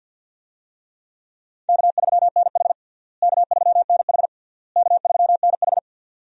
Morse code, heard as a keyed tone, about 51 wpm, 700 Hz. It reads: K3NH K3NH K3NH
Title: USA Call Signs - 4 Characters - Repeated 3x - 50wpm